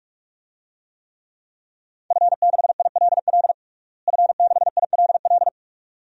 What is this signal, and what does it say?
Morse code, heard as a keyed tone, about 42 wpm, 700 Hz. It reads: F6ILL F6ILL
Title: DXCC-100 Call Signs - 5 Characters - Repeated 2x - 45wpm